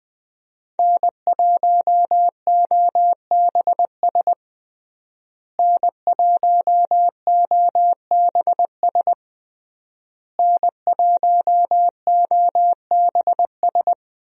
Morse code, heard as a keyed tone, about 20 wpm, 700 Hz. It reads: N1OBS N1OBS N1OBS
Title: USA Call Signs - 5 Characters - Repeated 3x - 20wpm